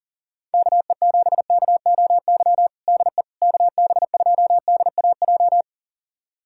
Morse code, heard as a keyed tone, about 37 wpm, 700 Hz. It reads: KE7KOY DE KB2DAJ
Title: USA Callsign DE Callsign - 6 Characters - 40wpm